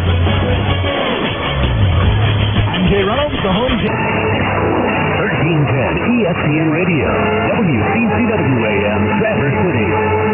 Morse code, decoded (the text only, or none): none